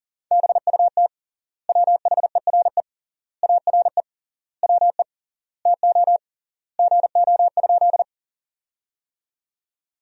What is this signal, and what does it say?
Morse code, heard as a keyed tone, about 40 wpm, 700 Hz. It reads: BUT WHERE ARE WE TO GO?
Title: 3x Word-Spacing - Sentences from Top 200 Words - 40wpm